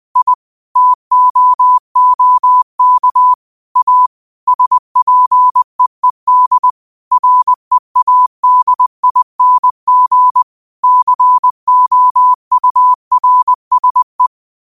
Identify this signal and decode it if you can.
Morse code, heard as a keyed tone, about 20 wpm, 1000 Hz. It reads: I TOOK A SPEED READING COURSE
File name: qotdfiles / 20wpm